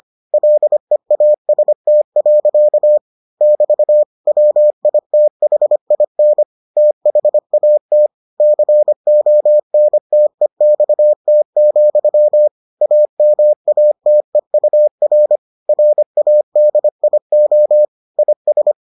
Morse code, heard as a keyed tone, about 25 wpm, 600 Hz. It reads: LEAST. <BT> WITHIN THAT CONTEXT, AMATEUR RADIO IS